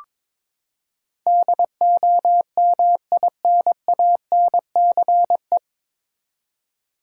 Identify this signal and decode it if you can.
Morse code, heard as a keyed tone, about 22 wpm, 700 Hz. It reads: DOMINANCE
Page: Morse Code Ninja practice 22wpm